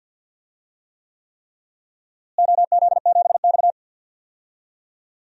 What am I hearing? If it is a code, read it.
Morse code, heard as a keyed tone, about 52 wpm, 700 Hz. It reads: OZ7X
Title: DXCC-100 Call Signs - 4 Characters - Repeated 3x - 50wpm